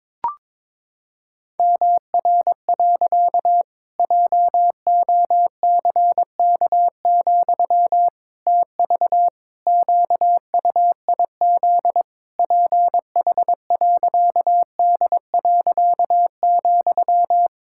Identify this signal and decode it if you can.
Morse code, heard as a keyed tone, about 22 wpm, 700 Hz. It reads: MR. JOCK, TV QUIZ PH.D.,
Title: Pangram Challenge v2 - 22wpm